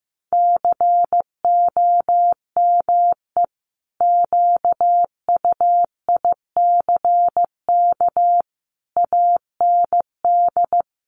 Morse code, heard as a keyed tone, about 15 wpm, 700 Hz. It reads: COME QUICK AND
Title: Sentences from Top 1300 Words 15wpm